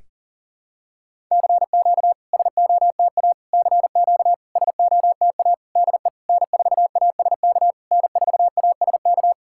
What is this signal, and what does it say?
Morse code, heard as a keyed tone, about 44 wpm, 690 Hz. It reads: CQ SOTA CQ SOTA DE N4ASK N4ASK
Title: SOTA QSO Practice 40wpm